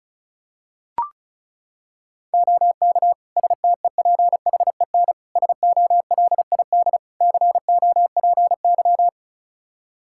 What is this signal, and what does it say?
Morse code, heard as a keyed tone, about 34 wpm, 700 Hz. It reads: OK STEPHEN SOLID COPY
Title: K1USN SST QSO Practice 35wpm